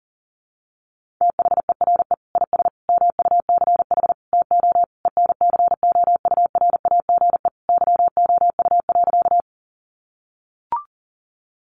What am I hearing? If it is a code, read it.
Morse code, heard as a keyed tone, about 40 wpm, 700 Hz. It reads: THERE IS MUCH TO ENCOURAGE YOU.
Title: Sentences from Top 1400 Words 40wpm